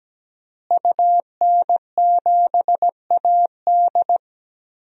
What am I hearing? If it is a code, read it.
Morse code, heard as a keyed tone, about 17 wpm, 700 Hz. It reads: UN7AD
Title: DXCC-100 Call Signs - 5 Characters - Repeated 3x - 17wpm